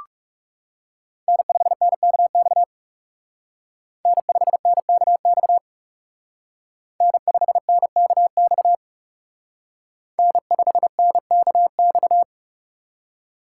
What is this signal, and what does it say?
Morse code, heard as a keyed tone, about 39 wpm, 700 Hz. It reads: N5NKX N5NKX N5NKX N5NKX